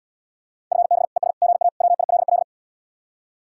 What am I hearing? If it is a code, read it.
Morse code, heard as a keyed tone, about 79 wpm, 700 Hz. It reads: FB ES GA WILL